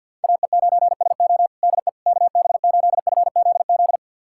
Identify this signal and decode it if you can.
Morse code, heard as a keyed tone, about 51 wpm, 700 Hz. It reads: AE9SO DE KB8FZZ
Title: USA Callsign DE Callsign 50wpm